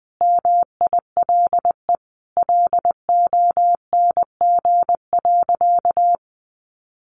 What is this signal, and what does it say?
Morse code, heard as a keyed tone, about 20 wpm, 700 Hz. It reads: MILE LONG.
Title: Sentences from Top 500 Words 20wpm